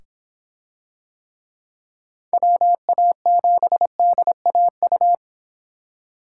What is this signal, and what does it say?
Morse code, heard as a keyed tone, about 25 wpm, 700 Hz. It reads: WA7DAU